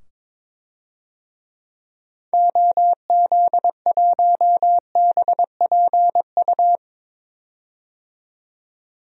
Morse code, heard as a keyed tone, about 22 wpm, 700 Hz. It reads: OZ1BPU
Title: DXCC-100 Call Signs - 6 Characters - 22wpm